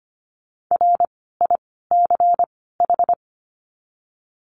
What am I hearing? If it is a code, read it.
Morse code, heard as a keyed tone, about 25 wpm, 700 Hz. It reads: R I C H
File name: course / 25:10wpm / 094 Lesson 16f - Letter c - Words 25wpm